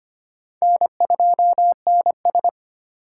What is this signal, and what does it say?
Morse code, heard as a keyed tone, about 25 wpm, 700 Hz. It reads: N2NS
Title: K1USN SST QSO Practice - 25wpm